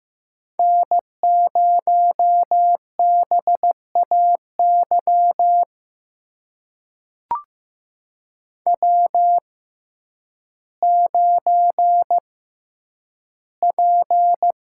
Morse code, heard as a keyed tone, about 15 wpm, spaced as for 9 wpm, 700 Hz. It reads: N0BAY W9P
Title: USA Call Signs - 5 Characters - 6x Character Spacing - 15wpm